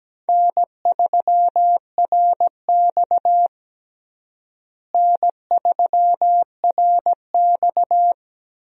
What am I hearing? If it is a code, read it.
Morse code, heard as a keyed tone, about 17 wpm, 700 Hz. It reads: N3RX N3RX
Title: USA Call Signs - 4 Characters - Repeated 3x - 17wpm